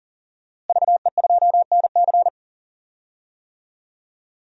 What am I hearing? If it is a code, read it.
Morse code, heard as a keyed tone, about 38 wpm, 700 Hz. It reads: VE2NC
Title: DXCC-100 Call Signs - 5 Characters - Repeated 2x - 40wpm